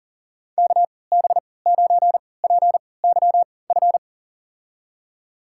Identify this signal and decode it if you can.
Morse code, heard as a keyed tone, about 40 wpm, spaced as for 23 wpm, 700 Hz. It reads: KB9PYF
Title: USA Call Signs - 6 Characters - 3x Character Spacing - 40wpm